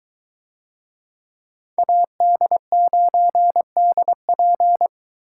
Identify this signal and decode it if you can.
Morse code, heard as a keyed tone, about 22 wpm, 700 Hz. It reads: AD9DP